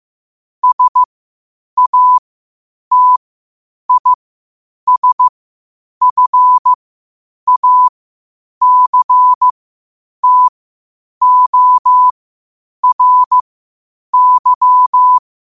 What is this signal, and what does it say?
Morse code, heard as a keyed tone, about 15 wpm, spaced as for 8 wpm, 1000 Hz. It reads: SATISFACTORY
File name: qotdfiles / 8wpm